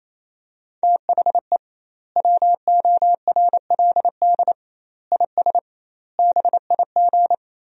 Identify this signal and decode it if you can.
Morse code, heard as a keyed tone, about 28 wpm, 700 Hz. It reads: THE WORLD IS BIG
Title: 2x Word-Spacing - Sentences from Top 400 Words - 28wpm